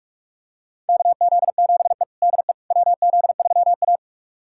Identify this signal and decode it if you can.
Morse code, heard as a keyed tone, about 44 wpm, 700 Hz. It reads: KZ7E DE WZ3A
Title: USA Callsign DE Callsign - 4 Characters - 45wpm